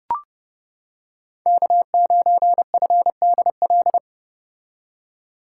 Morse code, heard as a keyed tone, about 29 wpm, 700 Hz. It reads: K9FDL